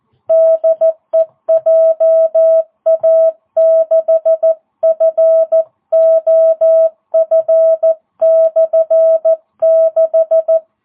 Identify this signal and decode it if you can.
Morse code, heard as a keyed tone, about 14 wpm, 660 Hz. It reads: DE JA6FOF/6